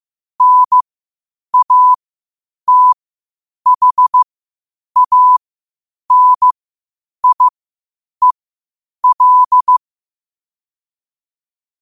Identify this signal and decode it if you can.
Morse code, heard as a keyed tone, about 15 wpm, spaced as for 8 wpm, 1000 Hz. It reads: NATHANIEL
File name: qotdfiles / 8wpm